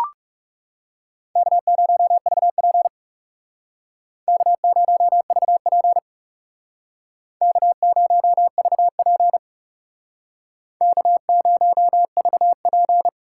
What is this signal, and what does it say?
Morse code, heard as a keyed tone, about 39 wpm, 700 Hz. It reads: K0VP K0VP K0VP K0VP